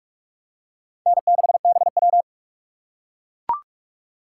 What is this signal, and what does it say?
Morse code, heard as a keyed tone, about 44 wpm, 700 Hz. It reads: N6BW
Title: USA Call Signs - 6x Character Spacing - 45wpm